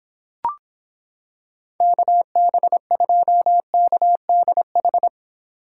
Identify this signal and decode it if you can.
Morse code, heard as a keyed tone, about 26 wpm, 700 Hz. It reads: KB2KDH